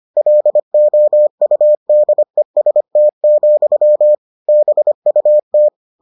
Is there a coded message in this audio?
Yes: Morse code from 0.2 to 5.7 s, heard as a keyed tone, about 25 wpm, 600 Hz. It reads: LOUDEST, BUT